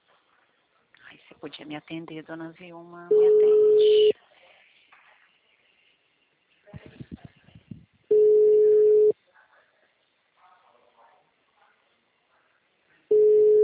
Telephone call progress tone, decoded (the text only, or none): ringback tone